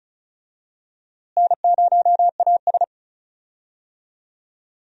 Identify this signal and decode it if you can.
Morse code, heard as a keyed tone, about 35 wpm, 700 Hz. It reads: N0AS
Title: USA Call Signs - 4 Characters - Repeated 2x - 35wpm